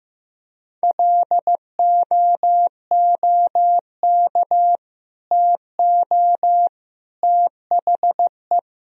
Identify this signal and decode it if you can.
Morse code, heard as a keyed tone, about 15 wpm, 700 Hz. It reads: LOOK TO THE
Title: Sentences from Top 1000 Words - Review Part 3 - 15wpm